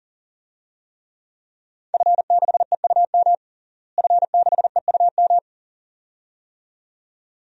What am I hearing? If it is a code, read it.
Morse code, heard as a keyed tone, about 40 wpm, 700 Hz. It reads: F6EUM F6EUM
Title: DXCC-100 Call Signs - 5 Characters - Repeated 2x - 40wpm